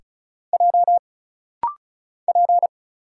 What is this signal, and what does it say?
Morse code, heard as a keyed tone, about 34 wpm, 700 Hz. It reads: J P
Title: Single Letters - Rapid-Fire 35wpm